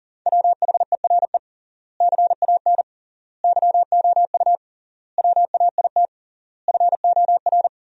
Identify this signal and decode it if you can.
Morse code, heard as a keyed tone, about 40 wpm, 700 Hz. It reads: WHERE CAN YOU WAIT FOR
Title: 3x Word-Spacing - Sentences from Top 500 Words - 40wpm